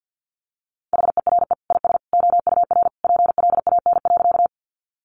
Morse code, heard as a keyed tone, about 50 wpm, 700 Hz. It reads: HERE IS OUR PLAN.
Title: Sentences from Top 500 Words 50wpm